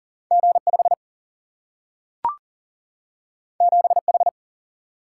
Morse code, heard as a keyed tone, about 40 wpm, 700 Hz. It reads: G5 7H
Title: Single-Letter Single-Number 40wpm